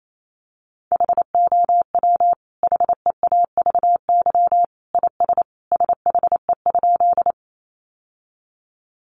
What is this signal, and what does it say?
Morse code, heard as a keyed tone, about 28 wpm, 700 Hz. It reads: HOW HEAVY IS SHE?
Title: Sentences from Top 1100 Words 28wpm